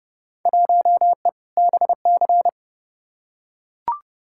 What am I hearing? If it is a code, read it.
Morse code, heard as a keyed tone, about 30 wpm, 700 Hz. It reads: 1E BC
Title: Field Day Exchanges - 30wpm